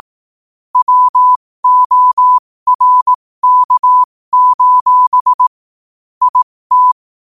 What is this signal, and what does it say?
Morse code, heard as a keyed tone, about 18 wpm, spaced as for 15 wpm, 1000 Hz. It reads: WORK: IT
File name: qotdfiles / 15wpm